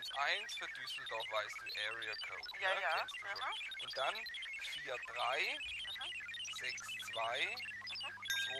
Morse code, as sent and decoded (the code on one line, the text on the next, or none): none